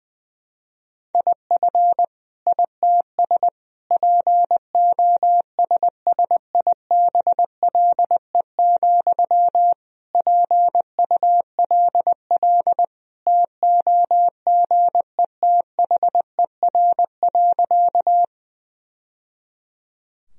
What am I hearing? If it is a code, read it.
Morse code, heard as a keyed tone, about 21 wpm, 700 Hz. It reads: IF ITS POSSIBLE, PULL TOGETHER.
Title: Sentences from Top 1000 Words - Review Part 2 - 20wpm